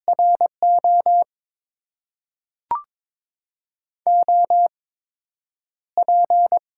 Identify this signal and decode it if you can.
Morse code, heard as a keyed tone, about 22 wpm, spaced as for 9 wpm, 700 Hz. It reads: RO OP
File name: Top 90 QSO Elements - 8x Character Spacing - 22wpm